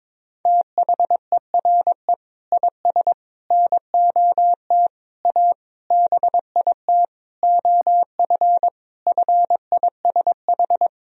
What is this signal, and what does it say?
Morse code, heard as a keyed tone, about 22 wpm, 700 Hz. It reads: THERE IS NOT A BIT OF FISH
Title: Sentences from Top 1000 Words - Review Part 1 - 22wpm